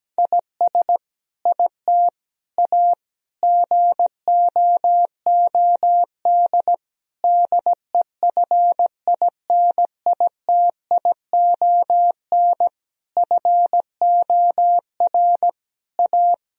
Morse code, heard as a keyed tone, about 17 wpm, 700 Hz. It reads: IS IT A GOOD DEFINITION FOR A
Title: Sentences from Top 1000 Words - Review Part 4 - 17wpm